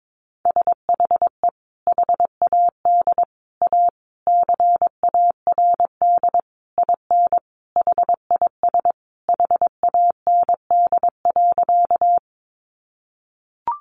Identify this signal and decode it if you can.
Morse code, heard as a keyed tone, about 22 wpm, 700 Hz. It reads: SHE HAD A CARD IN HIS HAND.
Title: Sentences from Top 600 Words 22wpm